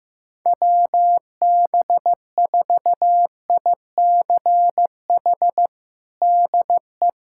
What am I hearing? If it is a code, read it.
Morse code, heard as a keyed tone, about 15 wpm, 700 Hz. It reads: WB4ICH DE
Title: USA Callsign DE Callsign - 6 Characters - 15wpm